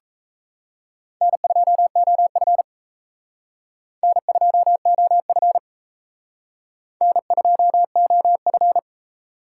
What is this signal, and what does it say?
Morse code, heard as a keyed tone, about 34 wpm, 700 Hz. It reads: N2OF N2OF N2OF
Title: Speed Racing - USA 1x2 Call Signs 28wpm